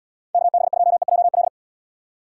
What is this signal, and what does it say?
Morse code, heard as a keyed tone, about 78 wpm, 700 Hz. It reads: KB2EJL